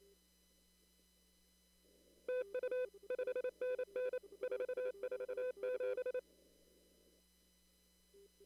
Morse code, heard as a keyed tone, about 27 wpm, 500 Hz. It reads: TU 5NN 447
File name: CW / kt5j wpx cw-op-k5pi